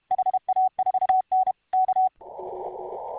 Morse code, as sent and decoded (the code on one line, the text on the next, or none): .... .- ....- -. / -.-
HA4N K